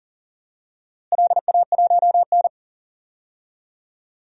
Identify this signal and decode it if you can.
Morse code, heard as a keyed tone, about 40 wpm, 700 Hz. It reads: LA1N
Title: DXCC-100 Call Signs - 4 Characters - Repeated 2x - 40wpm